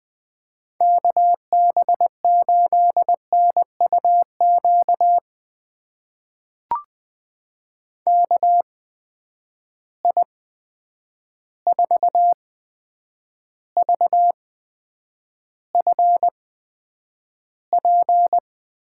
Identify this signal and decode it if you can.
Morse code, heard as a keyed tone, about 20 wpm, spaced as for 9 wpm, 700 Hz. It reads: KB8NUQ KI4VFP
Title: USA Call Signs - 6 Characters - 8x Character Spacing - 20wpm